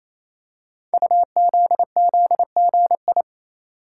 Morse code, heard as a keyed tone, about 27 wpm, 700 Hz. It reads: UZZGI
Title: Code Groups - 5 Letters - 28wpm